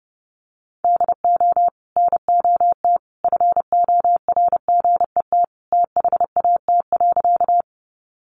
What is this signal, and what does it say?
Morse code, heard as a keyed tone, about 30 wpm, 700 Hz. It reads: DO NOT FORGET THAT.
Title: Sentences from Top 1200 Words 30wpm